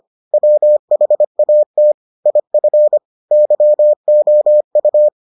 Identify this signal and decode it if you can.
Morse code, heard as a keyed tone, about 25 wpm, 600 Hz. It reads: WHAT IF YOU